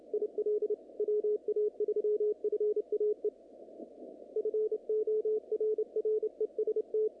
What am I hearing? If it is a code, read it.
Morse code, heard as a keyed tone, about 34 wpm, 440 Hz. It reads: IL WA3FAE FORREST
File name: contests / naqp-cw-k5tr-op-k5pi- -2200